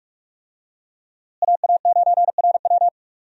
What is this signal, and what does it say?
Morse code, heard as a keyed tone, about 44 wpm, 700 Hz. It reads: AA9RW
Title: USA Call Signs - 5 Characters - Repeated 3x - 45wpm